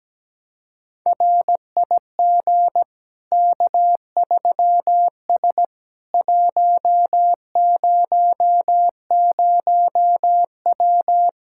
Morse code, heard as a keyed tone, about 17 wpm, 700 Hz. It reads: RIG K3S 100W
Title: K1USN SST QSO Practice 17wpm